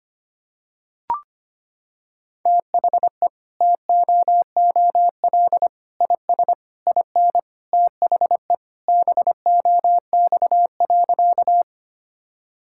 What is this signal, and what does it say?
Morse code, heard as a keyed tone, about 25 wpm, 700 Hz. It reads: THE TOOL IS IN THE BOX.
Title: Sentences from Top 1000 Words - Review Part 6 - 25wpm